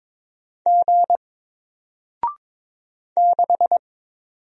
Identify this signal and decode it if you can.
Morse code, heard as a keyed tone, about 22 wpm, 700 Hz. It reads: G 6
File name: Single-Letter-Number-22wpm